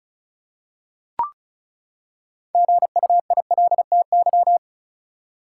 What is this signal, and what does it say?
Morse code, heard as a keyed tone, about 36 wpm, 700 Hz. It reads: GUILTY
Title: Six Letter Words - 35wpm